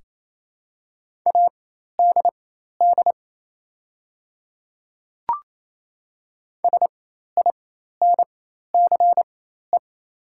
Morse code, heard as a keyed tone, about 28 wpm, spaced as for 13 wpm, 700 Hz. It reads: ADD SINCE
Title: 4x Character Spacing - Top 300 Words - 28wpm